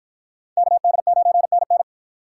Morse code, heard as a keyed tone, about 53 wpm, 700 Hz. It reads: KD9NN